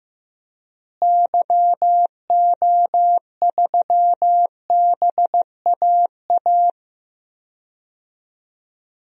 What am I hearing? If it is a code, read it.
Morse code, heard as a keyed tone, about 15 wpm, 700 Hz. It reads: YO3BAA
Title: DXCC-100 Call Signs - 6 Characters - Repeated 3x - 15wpm